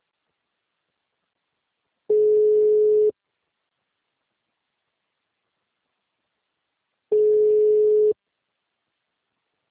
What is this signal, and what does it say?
Ringback tone (Europe, 425 Hz).